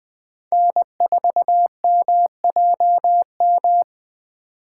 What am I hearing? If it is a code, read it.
Morse code, heard as a keyed tone, about 20 wpm, 700 Hz. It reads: N4MJM
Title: USA Call Signs - 5 Characters - 6x Character Spacing - 20wpm